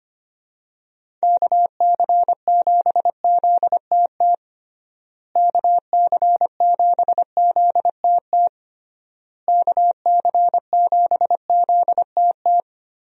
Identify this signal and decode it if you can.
Morse code, heard as a keyed tone, about 25 wpm, 700 Hz. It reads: KC7ZTT KC7ZTT KC7ZTT
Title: USA Call Signs - 6 Characters - Repeated 3x - 25wpm